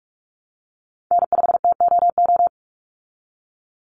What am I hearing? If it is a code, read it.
Morse code, heard as a keyed tone, about 44 wpm, 700 Hz. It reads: N5TOO
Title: Call Signs Repeated 3x - 45wpm